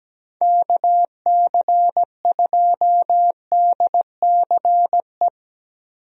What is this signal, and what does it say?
Morse code, heard as a keyed tone, about 17 wpm, 700 Hz. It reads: KC2DCE
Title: USA Call Signs - 6 Characters - Repeated 3x - 17wpm